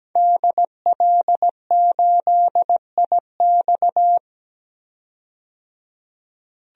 Morse code, heard as a keyed tone, about 17 wpm, 700 Hz. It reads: DL8IX
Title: DXCC-100 Call Signs - 5 Characters - 17wpm